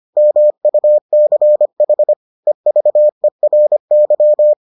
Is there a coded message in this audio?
Yes: Morse code from 0.2 to 4.5 s, heard as a keyed tone, about 25 wpm, 600 Hz. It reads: MUCH EVERY